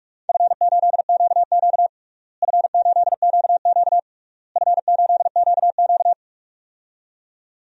Morse code, heard as a keyed tone, about 48 wpm, 700 Hz. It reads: F8QQ F8QQ F8QQ
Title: DXCC-100 Call Signs - 4 Characters - Repeated 3x - 45wpm